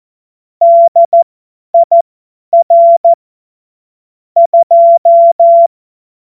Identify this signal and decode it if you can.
Morse code, heard as a keyed tone, about 14 wpm, spaced as for 10 wpm, 680 Hz. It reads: DIR 2
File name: Morsekurs / Witze / Witz 10. August 2021, 50 F70